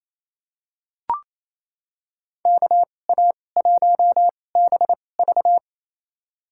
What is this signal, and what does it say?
Morse code, heard as a keyed tone, about 28 wpm, spaced as for 20 wpm, 700 Hz. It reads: KA1BV
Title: USA Call Signs - 5 Characters - 2x Character Spacing - 28wpm